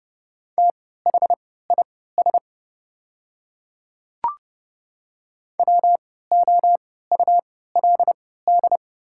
Morse code, heard as a keyed tone, about 30 wpm, spaced as for 17 wpm, 700 Hz. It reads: THIS WOULD
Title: 3x Character Spacing - Top 100 Words - 30wpm